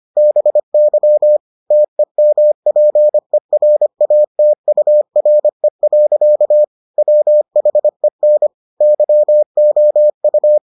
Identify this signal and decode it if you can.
Morse code, heard as a keyed tone, about 25 wpm, 600 Hz. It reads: BY TEMPERATURE. WHEN YOU